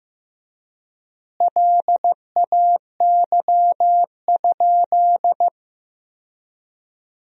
Morse code, heard as a keyed tone, about 15 wpm, 700 Hz. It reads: LAY?
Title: 3x Word-Spacing - Sentences from Top 400 Words - 15wpm